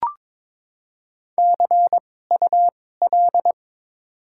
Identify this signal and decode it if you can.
Morse code, heard as a keyed tone, about 22 wpm, spaced as for 16 wpm, 700 Hz. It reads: CUL
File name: Top 90 QSO Elements - 2x Character Spacing - 22wpm